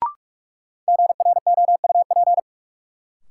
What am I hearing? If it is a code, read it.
Morse code, heard as a keyed tone, about 46 wpm, 700 Hz. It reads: GROUP
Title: Sentences from Top 1000 Words - Review Part 1 - New Word Drill - 45wpm